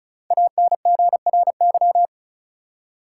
Morse code, heard as a keyed tone, about 34 wpm, 700 Hz. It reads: ANGRY
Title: Five Letter Words - 35wpm